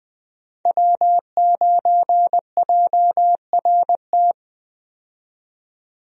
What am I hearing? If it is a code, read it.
Morse code, heard as a keyed tone, about 20 wpm, 700 Hz. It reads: W9JRT
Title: USA Call Signs - 5 Characters - 20wpm